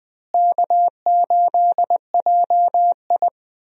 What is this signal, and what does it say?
Morse code, heard as a keyed tone, about 21 wpm, 700 Hz. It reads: K8JI
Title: USA Call Signs - 4 Characters - Repeated 3x - 20wpm